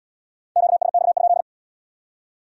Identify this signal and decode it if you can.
Morse code, heard as a keyed tone, about 75 wpm, 700 Hz. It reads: QSX?